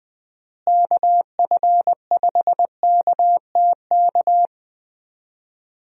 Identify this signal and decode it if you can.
Morse code, heard as a keyed tone, about 20 wpm, 700 Hz. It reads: KF5KTK
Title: USA Call Signs - 6 Characters - Repeated 3x - 20wpm